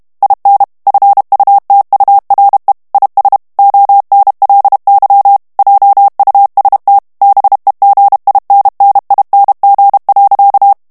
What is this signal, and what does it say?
Morse code, heard as a keyed tone, about 32 wpm, 800 Hz. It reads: IN FUTURE IS ONLY JUST BEGINNING.
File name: ANGLAIS texte clair 2 vitesse 30 mots minute